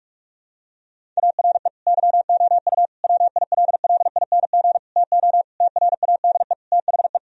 Morse code, heard as a keyed tone, about 46 wpm, 700 Hz. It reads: ARE YOU WILLING TO TRADE THE
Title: Sentences from Top 1000 Words - Review Part 4 - 45wpm